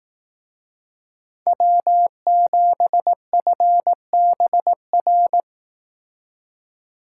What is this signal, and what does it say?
Morse code, heard as a keyed tone, about 18 wpm, 700 Hz. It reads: W7FBR
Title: Speed Racing - USA 1x3 Call Signs 15wpm